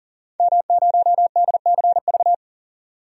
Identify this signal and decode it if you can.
Morse code, heard as a keyed tone, about 37 wpm, 700 Hz. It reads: M0DCV
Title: DXCC-100 Call Signs - 5 Characters - 40wpm